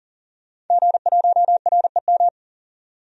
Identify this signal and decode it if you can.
Morse code, heard as a keyed tone, about 40 wpm, 700 Hz. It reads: G1REM
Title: DXCC-100 Call Signs - 5 Characters - 40wpm